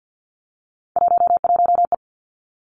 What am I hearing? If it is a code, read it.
Morse code, heard as a keyed tone, about 50 wpm, 700 Hz. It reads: 11E
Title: Three Challenging Letter-Number Combinations 50wpm